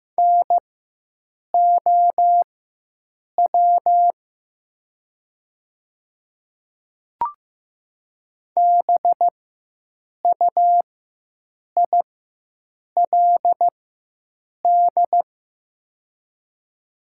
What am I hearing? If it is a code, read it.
Morse code, heard as a keyed tone, about 15 wpm, spaced as for 7 wpm, 700 Hz. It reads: NOW BUILD
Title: 4x Character Spacing - Top 200 Words - 15wpm